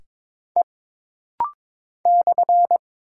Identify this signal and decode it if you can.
Morse code, heard as a keyed tone, about 22 wpm, 700 Hz. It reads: E /
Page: This rapid-fire timing is geared to push your ICR (Instant Character Recognition) skills.